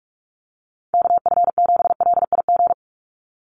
Morse code, heard as a keyed tone, about 44 wpm, 700 Hz. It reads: KF7LIG
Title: USA Call Signs 45wpm